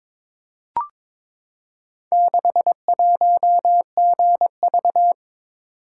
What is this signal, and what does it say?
Morse code, heard as a keyed tone, about 22 wpm, 700 Hz. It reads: 61GV
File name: practice / 22wpm / Four Letter-Number Combinations - 22wpm